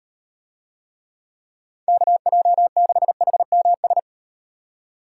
Morse code, heard as a keyed tone, about 38 wpm, 700 Hz. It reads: KJ6HMS